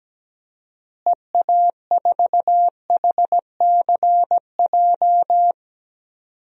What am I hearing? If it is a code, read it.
Morse code, heard as a keyed tone, about 17 wpm, 700 Hz. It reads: EA4HCJ
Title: DXCC-100 Call Signs - 6 Characters - Repeated 2x - 17wpm